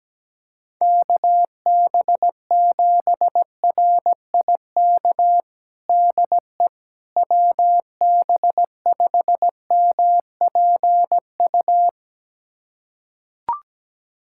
Morse code, heard as a keyed tone, about 17 wpm, 700 Hz. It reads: KB7RIK DE WB5MPU
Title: USA Callsign DE Callsign - 6 Characters - 17wpm